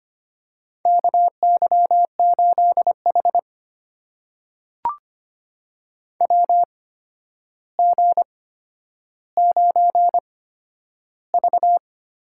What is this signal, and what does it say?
Morse code, heard as a keyed tone, about 25 wpm, spaced as for 11 wpm, 700 Hz. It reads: KY8H WG9V
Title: USA Call Signs - 4 Characters - 8x Character Spacing - 25wpm